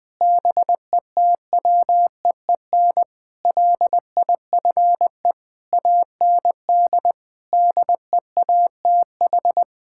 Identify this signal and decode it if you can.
Morse code, heard as a keyed tone, about 20 wpm, 700 Hz. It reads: BETWEEN LIFE AND DEATH
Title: Sets of 8 Words - 20wpm